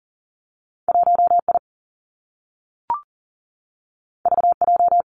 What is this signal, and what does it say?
Morse code, heard as a keyed tone, about 41 wpm, 700 Hz. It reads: 1 I V J
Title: Two Challenging Letter-Number Combinations 40wpm